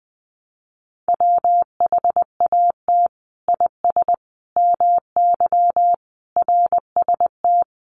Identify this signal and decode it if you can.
Morse code, heard as a keyed tone, about 20 wpm, 700 Hz. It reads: WHAT IS MY RST
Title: Common QSO Phrases 20wpm